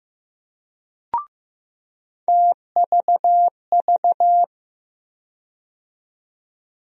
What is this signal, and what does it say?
Morse code, heard as a keyed tone, about 15 wpm, 700 Hz. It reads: TVV
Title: USA Call Sign Postfixes - 3 Letters 15wpm